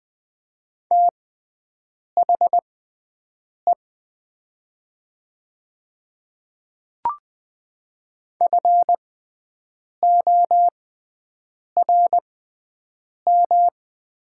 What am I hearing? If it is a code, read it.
Morse code, heard as a keyed tone, about 21 wpm, spaced as for 7 wpm, 700 Hz. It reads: THE FORM
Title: 6x Character Spacing - Top 200 Words - 20wpm